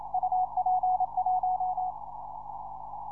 Morse code, heard as a keyed tone, about 27 wpm, 740 Hz. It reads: UP1